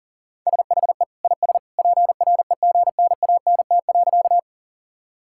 Morse code, heard as a keyed tone, about 40 wpm, 700 Hz. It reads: SHE IS PREGNANT.
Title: Sentences from Top 1000 Words - Review Part 6 - 40wpm